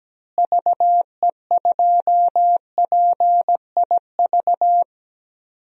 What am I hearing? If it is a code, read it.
Morse code, heard as a keyed tone, about 17 wpm, 700 Hz. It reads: VE2PIV
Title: DXCC-100 Call Signs - 6 Characters - Repeated 3x - 17wpm